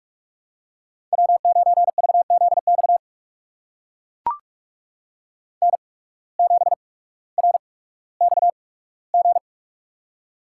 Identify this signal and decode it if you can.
Morse code, heard as a keyed tone, about 44 wpm, spaced as for 19 wpm, 700 Hz. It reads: W9VZX N7RXG
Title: USA Call Signs - 5 Characters - 8x Character Spacing - 45wpm